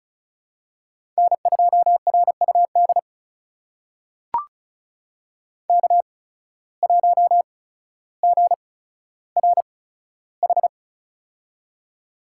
Morse code, heard as a keyed tone, about 36 wpm, spaced as for 15 wpm, 700 Hz. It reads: N2RUD K1GRH
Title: USA Call Signs - 8x Character Spacing - 35wpm